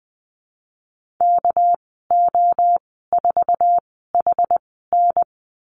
Morse code, heard as a keyed tone, about 20 wpm, spaced as for 15 wpm, 700 Hz. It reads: KO4HN
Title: 207 Lesson 33g - Letter j - Call Signs 20wpm